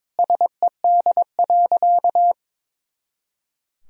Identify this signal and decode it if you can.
Morse code, heard as a keyed tone, about 22 wpm, 700 Hz. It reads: SED.
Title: Sentences from Top 100 Words and Abbreviations 22wpm